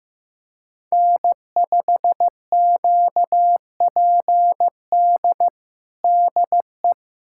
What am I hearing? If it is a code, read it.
Morse code, heard as a keyed tone, about 15 wpm, 700 Hz. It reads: N5QPD DE
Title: USA Callsign DE Callsign - 5 Characters - 15wpm